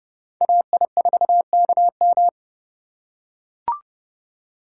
Morse code, heard as a keyed tone, about 30 wpm, 700 Hz. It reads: AI4KM
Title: USA Call Signs - 3x Character Spacing - 30wpm